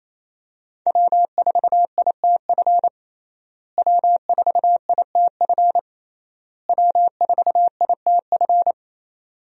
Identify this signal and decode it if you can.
Morse code, heard as a keyed tone, about 27 wpm, 700 Hz. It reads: W4ITF W4ITF W4ITF
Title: USA Call Signs - 5 Characters - Repeated 3x - 28wpm